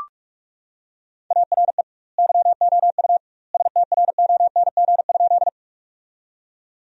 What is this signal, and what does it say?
Morse code, heard as a keyed tone, about 46 wpm, 700 Hz. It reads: ARE YOU STRONG?
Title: 2x Word-Spacing - Sentences from Top 500 Words - 45wpm